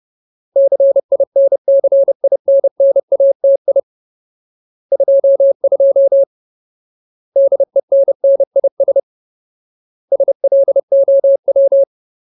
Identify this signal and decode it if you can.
Morse code, heard as a keyed tone, about 30 wpm, 550 Hz. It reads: CINCINNATI 22 DENNIS SLOW
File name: Ultimate Word List / 30WPM Ultimate